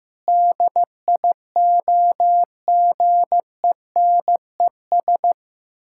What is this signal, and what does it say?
Morse code, heard as a keyed tone, about 15 wpm, 700 Hz. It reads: DIOGENES
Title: Pangram Challenge v2 - 15wpm